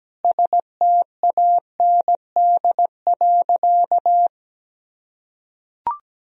Morse code, heard as a keyed tone, about 17 wpm, 700 Hz. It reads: STAND.
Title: Sentences from Top 1000 Words - Review Part 1 - 17wpm